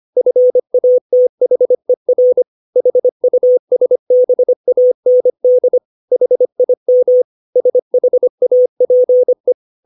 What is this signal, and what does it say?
Morse code, heard as a keyed tone, about 25 wpm, 500 Hz. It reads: FATHER HUSBAND HIM SHAPE
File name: 632 words / 632 Words 25WPM